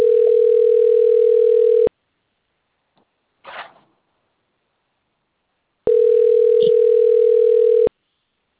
Ringback tone (North America), 440 + 480 Hz.